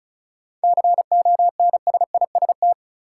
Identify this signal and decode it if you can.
Morse code, heard as a keyed tone, about 36 wpm, 700 Hz. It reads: CONSIST
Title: Sentences from Top 1400 Words - New Word Drill Alt - 35wpm